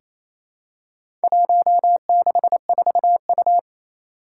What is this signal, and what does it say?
Morse code, heard as a keyed tone, about 28 wpm, 700 Hz. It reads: 164U
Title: Four Challenging Letter-Number Combinations 28wpm